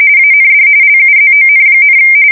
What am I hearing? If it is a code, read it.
RTTY, about 45.45 baud. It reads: AND LUZON,
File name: fsk170 / battleshipnj